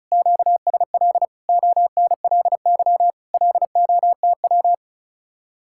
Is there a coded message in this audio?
Yes: Morse code from 0.1 to 4.7 s, heard as a keyed tone, about 44 wpm, 700 Hz. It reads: QSL ONLY LOTW